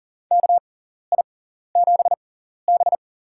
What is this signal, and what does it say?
Morse code, heard as a keyed tone, about 40 wpm, spaced as for 14 wpm, 700 Hz. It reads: KI7B
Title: USA Call Signs - 4 Characters - 6x Character Spacing - 40wpm